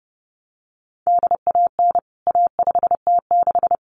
Morse code, heard as a keyed tone, about 30 wpm, 700 Hz. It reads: DAN A5T6
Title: CWT Exchange 30wpm